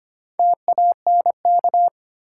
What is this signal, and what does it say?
Morse code, heard as a keyed tone, about 25 wpm, 700 Hz. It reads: TANK
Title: Four Letter Words - 25wpm